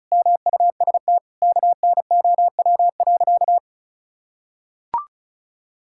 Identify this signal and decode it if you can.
Morse code, heard as a keyed tone, about 36 wpm, 700 Hz. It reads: MUST KNOW.
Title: Sentences from Top 1000 Words - Review Part 6 - 35wpm